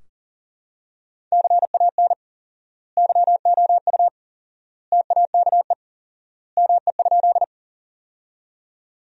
Morse code, heard as a keyed tone, about 40 wpm, 700 Hz. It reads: CAN YOU TAKE ME?
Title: 4x Word-Spacing - Sentences from Top 200 Words - 40wpm